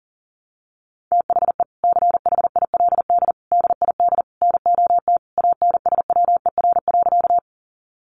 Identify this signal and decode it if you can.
Morse code, heard as a keyed tone, about 40 wpm, 700 Hz. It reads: THE CHILD DID NOT ANSWER.
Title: Sentences from Top 800 Words 40wpm